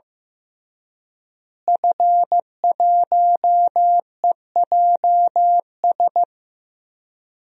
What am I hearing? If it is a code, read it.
Morse code, heard as a keyed tone, about 15 wpm, 700 Hz. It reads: F1EJS
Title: DXCC-100 Call Signs - 5 Characters - Repeated 3x - 15wpm